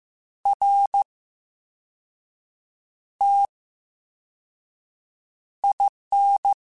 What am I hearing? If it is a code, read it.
Morse code, heard as a keyed tone, about 15 wpm, spaced as for 5 wpm, 800 Hz. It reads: RTIN
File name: Special INGs / Special INGs-mp3s / Slow ing @13